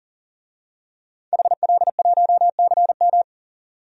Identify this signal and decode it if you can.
Morse code, heard as a keyed tone, about 40 wpm, 700 Hz. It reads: HL1CM
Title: DXCC-100 Call Signs - 5 Characters - 40wpm